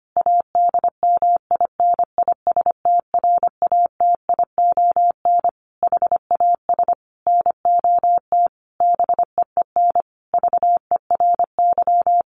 Morse code, heard as a keyed tone, about 25 wpm, 700 Hz. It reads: ADMINISTRATION HAS NOT BEEN VERY
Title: Sentences from Top 1300 Words 25wpm